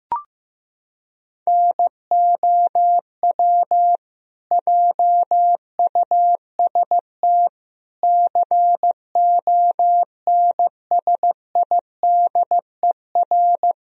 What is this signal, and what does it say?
Morse code, heard as a keyed tone, about 15 wpm, 700 Hz. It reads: NOW JUST CONSIDER
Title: Sentences from Top 1000 Words - Review Part 4 - 15wpm